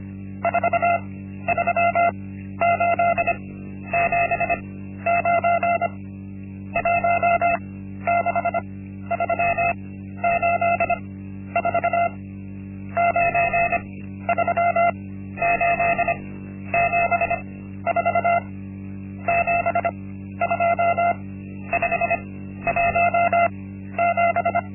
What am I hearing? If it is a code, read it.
Morse code, heard as a keyed tone, about 25 wpm, spaced as for 12 wpm, 690 Hz. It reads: 43879 16384 93874 72517
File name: Linksmakalnis-1